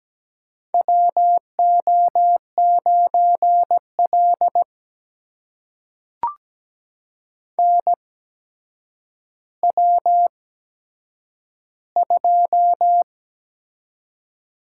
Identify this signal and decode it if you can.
Morse code, heard as a keyed tone, about 17 wpm, spaced as for 8 wpm, 700 Hz. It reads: WO9L NW2
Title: USA Call Signs - 4 Characters - 8x Character Spacing - 17wpm